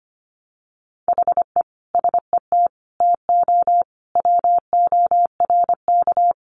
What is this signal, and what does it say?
Morse code, heard as a keyed tone, about 25 wpm, 700 Hz. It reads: HE SET TO WORK
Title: Sentences from Top 1200 Words 25wpm